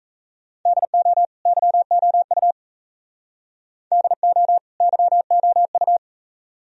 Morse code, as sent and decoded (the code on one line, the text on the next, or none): -.. --- / -.-- --- ..- / -.. --- / -.-- --- ..-
DO YOU DO YOU